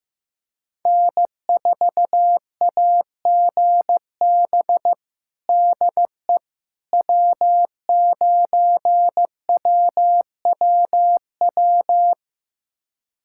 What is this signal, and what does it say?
Morse code, heard as a keyed tone, about 16 wpm, 700 Hz. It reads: N4AGB DE W9WWW
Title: USA Callsign DE Callsign - 5 Characters - 15wpm